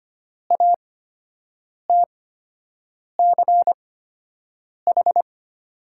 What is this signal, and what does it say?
Morse code, heard as a keyed tone, about 25 wpm, spaced as for 7 wpm, 700 Hz. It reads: ATCH
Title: 8x Character Spacing - Top 500 Words - 25wpm